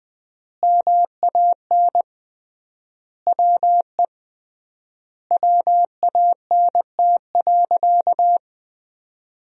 Morse code, heard as a keyed tone, about 20 wpm, 700 Hz. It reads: MAN WE WANT.
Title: 3x Word-Spacing - Sentences from Top 200 Words - 20wpm